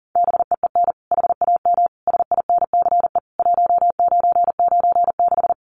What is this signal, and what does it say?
Morse code, heard as a keyed tone, about 40 wpm, 700 Hz. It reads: BEEN HAM SINCE 1996
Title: Common QSO Phrases 40wpm